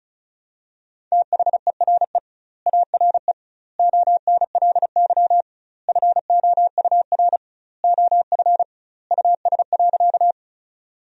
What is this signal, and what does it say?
Morse code, heard as a keyed tone, about 36 wpm, 700 Hz. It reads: THERE ARE ONLY FOUR OF US.
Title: 2x Word-Spacing - Sentences from Top 300 Words - 35wpm